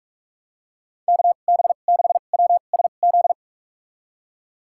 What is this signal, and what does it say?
Morse code, heard as a keyed tone, about 43 wpm, spaced as for 32 wpm, 700 Hz. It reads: KB6WSZ
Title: USA Call Signs - 6 Characters - 2x Character Spacing - 45wpm